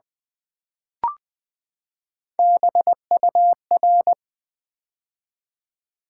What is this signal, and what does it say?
Morse code, heard as a keyed tone, about 20 wpm, 700 Hz. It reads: BUR
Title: Build A Word - 3 Letters At A Time - 6-Letter Words - 20wpm